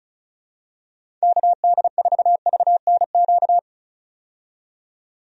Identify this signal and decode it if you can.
Morse code, heard as a keyed tone, about 36 wpm, 700 Hz. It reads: KD4VNQ
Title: USA Call Signs - 6 Characters - 35wpm